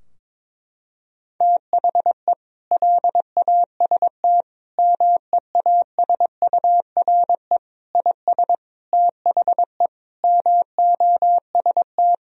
Morse code, heard as a keyed tone, about 22 wpm, 700 Hz. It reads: THE LAST MEASURE IS THE MOST
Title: Sentences from Top 1000 Words - Review Part 2 - 22wpm